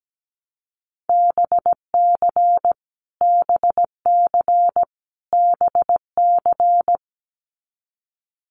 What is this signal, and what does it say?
Morse code, heard as a keyed tone, about 17 wpm, 700 Hz. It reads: BC BC BC
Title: Canadian Provinces and Territories - Repeated 3x - 17wpm